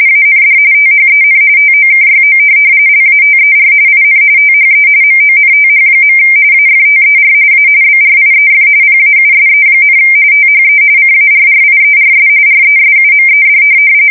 RTTY, about 45.45 baud. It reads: AND THEN TOKYO LATER IN THE MONTH. AFTER A BRIEF PAUSE, THE CARRIERS RESUMED
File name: fsk170 / USS-Missouri